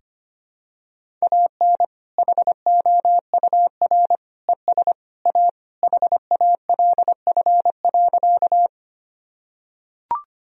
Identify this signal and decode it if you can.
Morse code, heard as a keyed tone, about 25 wpm, 700 Hz. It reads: AN HOUR ES A HALF.
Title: Sentences from Top 400 Words and Abbreviations 25wpm